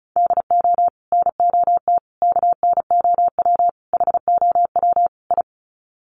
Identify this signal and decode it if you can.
Morse code, heard as a keyed tone, about 34 wpm, 700 Hz. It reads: DO NOT KNOW HOW I
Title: Sets of 5 Words Encore2 35wpm